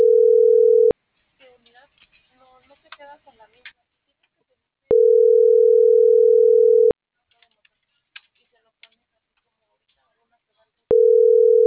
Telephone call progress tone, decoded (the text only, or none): ringback tone